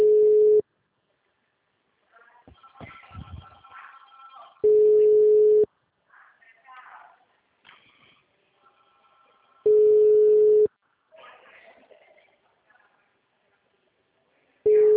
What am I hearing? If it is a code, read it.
Ringback tone (Europe, 425 Hz), with speech.